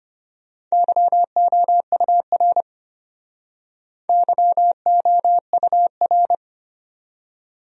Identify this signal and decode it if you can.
Morse code, heard as a keyed tone, about 34 wpm, 700 Hz. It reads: YOUR YOUR